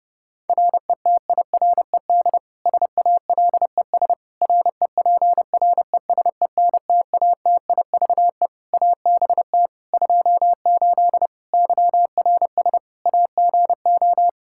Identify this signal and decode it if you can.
Morse code, heard as a keyed tone, about 30 wpm, 700 Hz. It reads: RETIRED SALES REPRESENTATIVE ABT 28 YRS AGO
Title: Practice QSO - Exchange 3 - More Info - 30wpm